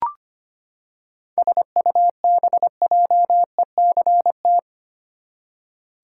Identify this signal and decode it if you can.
Morse code, heard as a keyed tone, about 25 wpm, 700 Hz. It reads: SUBJECT
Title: Seven Letter Words - 25wpm